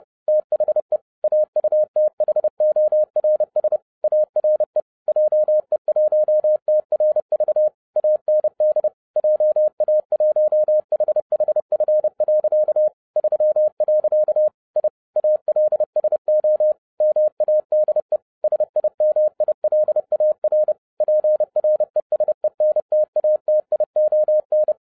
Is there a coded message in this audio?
Yes: Morse code from 0.3 to 24.7 s, heard as a keyed tone, about 30 wpm, 610 Hz. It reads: THE AUTHORS ARE JE1TRV AND JA1HHF. 3. I ALSO MADE SIMILAR PRESENTATION